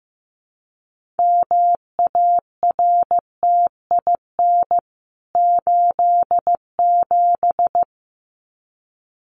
Morse code, heard as a keyed tone, about 15 wpm, 700 Hz. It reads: MARTIN 87
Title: CWT Exchange 15wpm